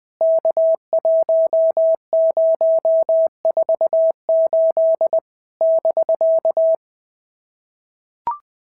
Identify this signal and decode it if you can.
Morse code, heard as a keyed tone, about 20 wpm, 650 Hz. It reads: K1048 <BK>
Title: POTA QSO Practice - 20wpm